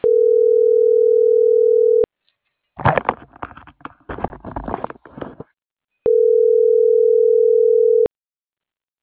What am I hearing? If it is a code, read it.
Ringback tone (North America), 440 + 480 Hz.